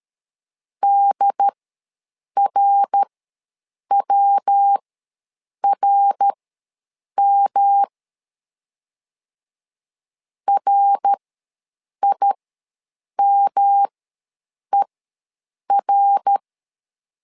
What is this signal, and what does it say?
Morse code, heard as a keyed tone, about 13 wpm, spaced as for 7 wpm, 800 Hz. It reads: DRWRM RIMER
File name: LEZIONI / 3 MWRO / 3 MWR ws 12wpm 1809